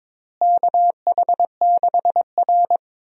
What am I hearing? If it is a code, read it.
Morse code, heard as a keyed tone, about 22 wpm, 700 Hz. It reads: KH6R
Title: USA Call Signs - 4 Characters - Repeated 3x - 22wpm